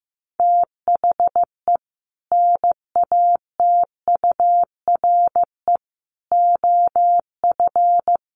Morse code, heard as a keyed tone, about 15 wpm, 700 Hz. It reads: THE NATURE OF
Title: Sets of 3 Words 15wpm